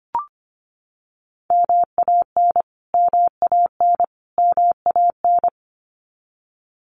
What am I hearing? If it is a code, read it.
Morse code, heard as a keyed tone, about 25 wpm, 700 Hz. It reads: MAN MAN MAN
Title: Top 500 Words Repeated 25wpm